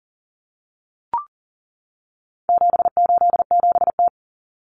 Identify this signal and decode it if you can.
Morse code, heard as a keyed tone, about 40 wpm, 700 Hz. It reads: 787T
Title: Common Cut Numbers - 4-Digit 40wpm